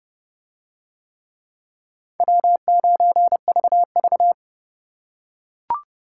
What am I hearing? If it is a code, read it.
Morse code, heard as a keyed tone, about 30 wpm, 700 Hz. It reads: W9VV